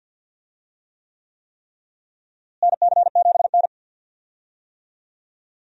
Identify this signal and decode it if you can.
Morse code, heard as a keyed tone, about 51 wpm, 700 Hz. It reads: NC7N
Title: USA Call Signs - 4 Characters - Repeated 3x - 50wpm